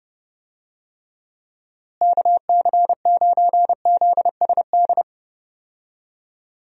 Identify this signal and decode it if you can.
Morse code, heard as a keyed tone, about 30 wpm, 700 Hz. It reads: KC9ZSD
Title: USA Call Signs - 6 Characters - Repeated 3x - 30wpm